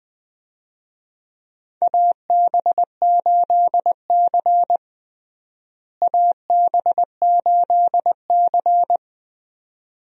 Morse code, heard as a keyed tone, about 20 wpm, 700 Hz. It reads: AB8C AB8C
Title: USA Call Signs - 4 Characters - Repeated 2x - 20wpm